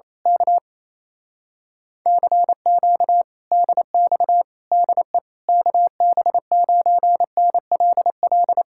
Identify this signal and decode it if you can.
Morse code, heard as a keyed tone, about 27 wpm, 700 Hz. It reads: K CQ DX DE KB9NLL
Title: Speed-Racing Common QSO Phrases - 28wpm